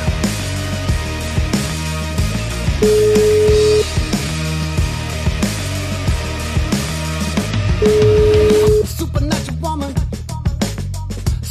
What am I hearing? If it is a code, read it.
Ringback tone (Europe, 425 Hz), with noise.